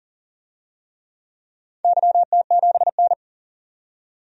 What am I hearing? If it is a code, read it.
Morse code, heard as a keyed tone, about 40 wpm, 690 Hz. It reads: YT7N
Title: DXCC-100 Call Signs - 4 Characters - 40wpm